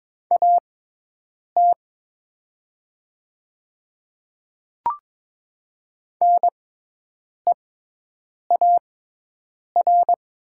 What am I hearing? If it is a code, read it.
Morse code, heard as a keyed tone, about 22 wpm, spaced as for 8 wpm, 700 Hz. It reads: AT NEAR
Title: 6x Character Spacing - Top 200 Words - 22wpm